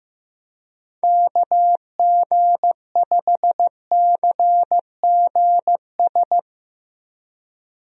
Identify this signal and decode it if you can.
Morse code, heard as a keyed tone, about 15 wpm, 700 Hz. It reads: KG5CGS
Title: USA Call Signs - 6 Characters - Repeated 3x - 15wpm